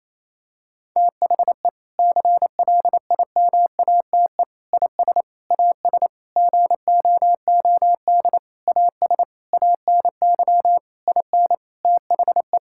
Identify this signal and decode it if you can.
Morse code, heard as a keyed tone, about 28 wpm, 700 Hz. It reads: THE CLIMATE IS AS GOOD AS ANY IN THE
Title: Sentences from Top 1600 Words 28wpm